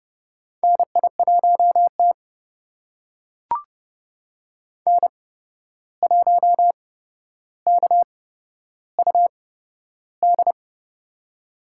Morse code, heard as a keyed tone, about 30 wpm, spaced as for 12 wpm, 700 Hz. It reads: NI1T N1KUD
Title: USA Call Signs - 8x Character Spacing - 30wpm